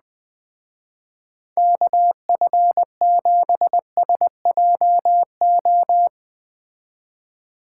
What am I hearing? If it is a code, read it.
Morse code, heard as a keyed tone, about 19 wpm, 700 Hz. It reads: KF7SJO
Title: Speed Racing - USA 2x3 Call Signs 17wpm